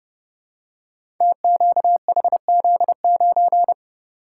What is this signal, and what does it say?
Morse code, heard as a keyed tone, about 30 wpm, 700 Hz. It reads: TQHZ9
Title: Code Groups - 5 Letter:Numbers - 30wpm